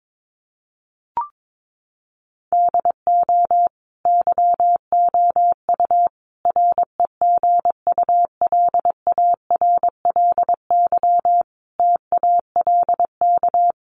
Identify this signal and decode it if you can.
Morse code, heard as a keyed tone, about 22 wpm, 700 Hz. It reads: DO YOU REGULARLY TALK
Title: Sentences from Top 1400 Words 22wpm